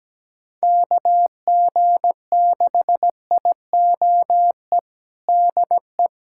Morse code, heard as a keyed tone, about 17 wpm, 700 Hz. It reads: KG6IOE DE
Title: USA Callsign DE Callsign 17wpm